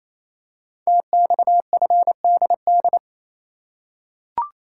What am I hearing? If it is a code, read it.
Morse code, heard as a keyed tone, about 28 wpm, 700 Hz. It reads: TXFDD
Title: Code Groups - 5 Letters - 28wpm